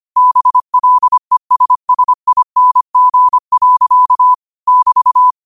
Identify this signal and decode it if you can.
Morse code, heard as a keyed tone, about 25 wpm, 1000 Hz. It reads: BLESSING. <BT>